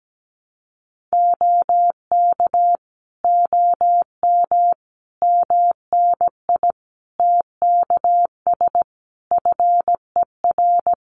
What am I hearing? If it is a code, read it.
Morse code, heard as a keyed tone, about 17 wpm, 700 Hz. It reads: OK OM MNI TKS FER
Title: Common QSO Phrases 17wpm